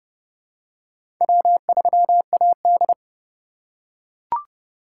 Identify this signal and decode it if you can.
Morse code, heard as a keyed tone, about 30 wpm, 700 Hz. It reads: W3AD
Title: USA Call Signs - 8x Character Spacing - 30wpm